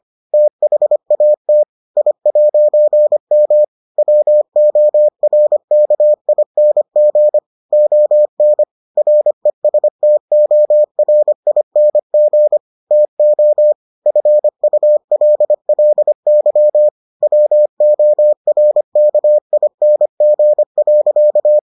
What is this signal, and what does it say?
Morse code, heard as a keyed tone, about 25 wpm, 600 Hz. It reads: THAT I'M WORKING ON RESTORING TO FULLY WORKING.